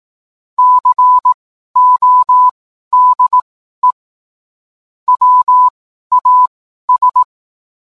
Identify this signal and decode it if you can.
Morse code, heard as a keyed tone, about 18 wpm, spaced as for 12 wpm, 1000 Hz. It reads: CODE WAS
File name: cwblog / cw mp3 directory / mem morse 12